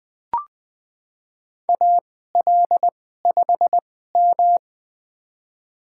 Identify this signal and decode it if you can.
Morse code, heard as a keyed tone, about 20 wpm, spaced as for 15 wpm, 700 Hz. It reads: AL5M
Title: USA Call Signs - 4 Characters - 2x Character Spacing - 20wpm